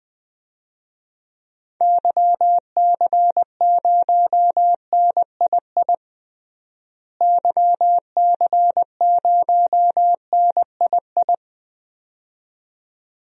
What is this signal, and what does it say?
Morse code, heard as a keyed tone, about 20 wpm, 700 Hz. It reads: YC0NII YC0NII
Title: DXCC-100 Call Signs - 6 Characters - Repeated 2x - 20wpm